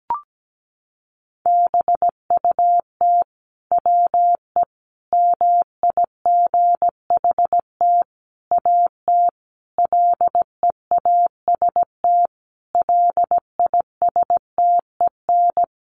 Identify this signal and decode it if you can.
Morse code, heard as a keyed tone, about 17 wpm, 700 Hz. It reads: BUT WE MIGHT AT LEAST LISTEN
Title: Sentences from Top 1300 Words 17wpm